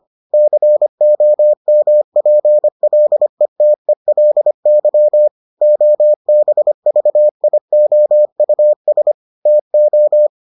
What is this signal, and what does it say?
Morse code, heard as a keyed tone, about 25 wpm, 600 Hz. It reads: COMPLETELY OBVIOUS TO